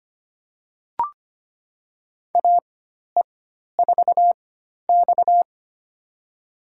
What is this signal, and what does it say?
Morse code, heard as a keyed tone, about 25 wpm, spaced as for 12 wpm, 700 Hz. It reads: AE4X
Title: USA Call Signs - 4 Characters - 4x Character Spacing - 25wpm